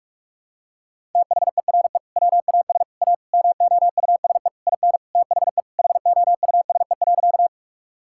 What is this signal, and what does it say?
Morse code, heard as a keyed tone, about 46 wpm, 700 Hz. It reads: THERE WAS A MOUSE IN THE HOUSE.
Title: Sentences from Top 1000 Words - Review Part 5 - 45wpm